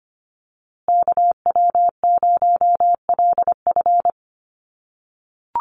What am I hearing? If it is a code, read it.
Morse code, heard as a keyed tone, about 24 wpm, 700 Hz. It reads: KW0LF
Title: USA Call Signs Repeated 3x 25wpm